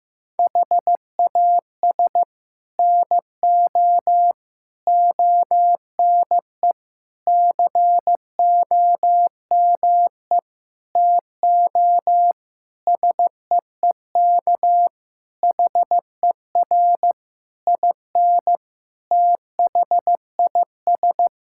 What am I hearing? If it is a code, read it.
Morse code, heard as a keyed tone, about 15 wpm, 700 Hz. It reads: HAS NO ONE COME TO SEEK HER IN THIS
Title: Sentences from Top 1600 Words 15wpm